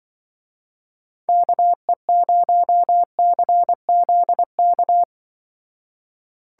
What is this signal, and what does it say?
Morse code, heard as a keyed tone, about 24 wpm, 700 Hz. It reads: KE0CZK
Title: Speed Racing - USA 2x3 Call Signs 20wpm